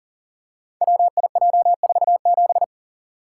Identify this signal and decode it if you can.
Morse code, heard as a keyed tone, about 41 wpm, 700 Hz. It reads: WIJ47
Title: Code Groups - 5 Letter:Numbers - 40wpm